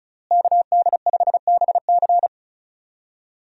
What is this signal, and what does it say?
Morse code, heard as a keyed tone, about 36 wpm, 700 Hz. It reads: KD5BC
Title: USA Call Signs - 5 Characters - Repeated 2x - 35wpm